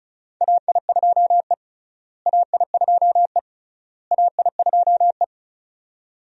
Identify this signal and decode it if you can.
Morse code, heard as a keyed tone, about 34 wpm, 700 Hz. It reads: AI2E AI2E AI2E
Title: USA Call Signs - 4 Characters - Repeated 3x - 35wpm